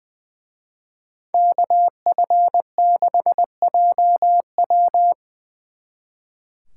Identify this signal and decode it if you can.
Morse code, heard as a keyed tone, about 25 wpm, 700 Hz. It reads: KF6JW
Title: Speed Racing - USA 2x2 Call Signs 20wpm